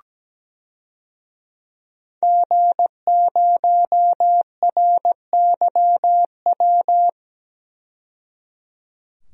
Morse code, heard as a keyed tone, about 17 wpm, 700 Hz. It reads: G0RYW
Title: DXCC-100 Call Signs - 5 Characters - 17wpm